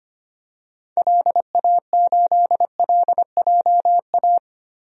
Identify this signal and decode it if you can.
Morse code, heard as a keyed tone, about 25 wpm, 700 Hz. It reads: LA8LJA
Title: DXCC-100 Call Signs - 6 Characters - Repeated 2x - 25wpm